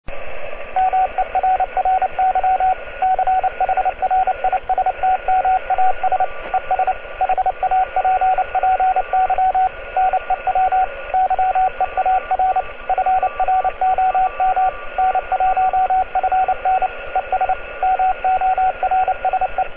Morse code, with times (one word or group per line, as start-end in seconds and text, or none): 0.8-2.7 MERRY
3.0-6.2 CHRISTMAS
6.5-6.9 ES
7.2-9.7 HAPPY
10.0-10.8 NEW
11.1-12.6 YEAR
12.9-14.7 FROM
15.0-16.9 N1FN
17.2-17.5 ES
17.8-19.6 MORSE